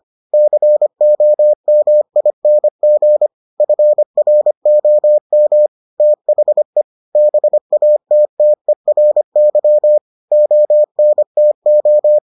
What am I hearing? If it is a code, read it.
Morse code, heard as a keyed tone, about 25 wpm, 600 Hz. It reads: COMING FROM THE BATTERY ONTO